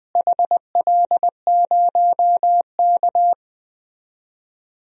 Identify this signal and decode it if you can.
Morse code, heard as a keyed tone, about 20 wpm, 700 Hz. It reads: HL0K
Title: DXCC-100 Call Signs - 4 Characters - Repeated 3x - 20wpm